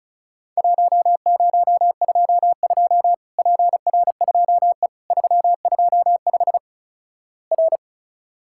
Morse code, heard as a keyed tone, about 36 wpm, 700 Hz. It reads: 1022 PR2E 325 R
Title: Worked All Europe DX Contest - QTC Exchange - 35wpm